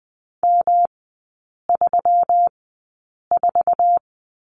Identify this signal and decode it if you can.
Morse code, heard as a keyed tone, about 20 wpm, 700 Hz. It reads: M 3 4
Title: 121 Lesson 20e - Number 3 - Three Characters 20wpm